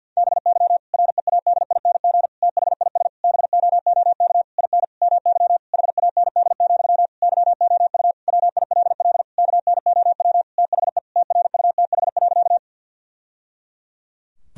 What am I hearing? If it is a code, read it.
Morse code, heard as a keyed tone, about 50 wpm, 700 Hz. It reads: BY READING THIS BOOK IN MY HAND, YOU WILL KNOW THE TRUTH.
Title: Sentences from Top 1000 Words - Review Part 3 - 50wpm